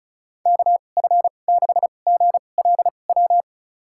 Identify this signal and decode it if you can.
Morse code, heard as a keyed tone, about 36 wpm, spaced as for 25 wpm, 700 Hz. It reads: KF6GLW
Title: USA Call Signs - 6 Characters - 2x Character Spacing - 35wpm